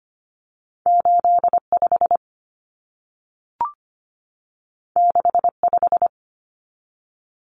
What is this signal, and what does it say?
Morse code, heard as a keyed tone, about 25 wpm, 700 Hz. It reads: 85 65
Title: Common Cut Numbers - 2-Digit 25wpm